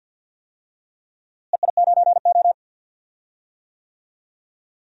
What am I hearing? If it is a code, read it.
Morse code, heard as a keyed tone, about 49 wpm, 700 Hz. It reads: EI9O
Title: DXCC-100 Call Signs - 4 Characters - Repeated 3x - 50wpm